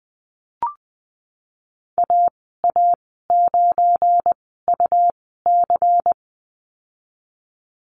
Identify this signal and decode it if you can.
Morse code, heard as a keyed tone, about 20 wpm, spaced as for 14 wpm, 700 Hz. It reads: AA9UC
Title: 173 Lesson 28g - Number 9 - Call Signs 20wpm